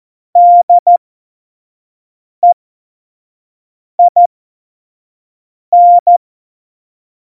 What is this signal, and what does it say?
Morse code, heard as a keyed tone, about 14 wpm, spaced as for 5 wpm, 700 Hz. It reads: DEIN
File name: Morsekurs / Witze / Witz 19. Dezember 2017, 25 F70